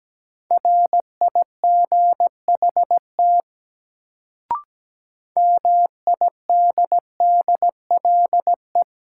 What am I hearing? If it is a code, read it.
Morse code, heard as a keyed tone, about 17 wpm, 700 Hz. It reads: RIGHT MIDDLE
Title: Sets of 2 Words - New Word Drill - 17wpm